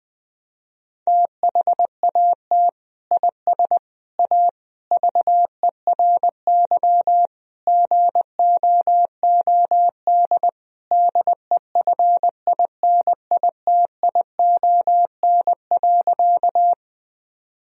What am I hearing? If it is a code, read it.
Morse code, heard as a keyed tone, about 20 wpm, 700 Hz. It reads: THAT IS A VERY GOOD DEFINITION.
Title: Sentences from Top 1000 Words - Review Part 4 - 20wpm